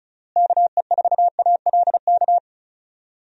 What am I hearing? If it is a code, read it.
Morse code, heard as a keyed tone, about 36 wpm, 700 Hz. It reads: KE4ALK
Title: USA Call Signs - 6 Characters - Repeated 3x - 35wpm